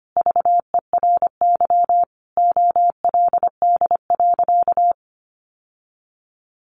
Morse code, heard as a keyed tone, about 25 wpm, 700 Hz. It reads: VERY OLD.
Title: Sentences from Top 1100 Words 25wpm